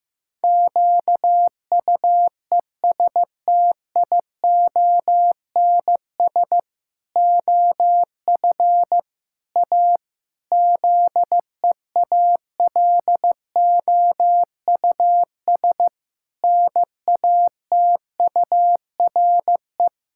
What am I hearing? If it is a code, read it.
Morse code, heard as a keyed tone, about 15 wpm, 700 Hz. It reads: QUESTIONS OF A ZEALOUS NATURE
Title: Pangram Challenge v2 - 15wpm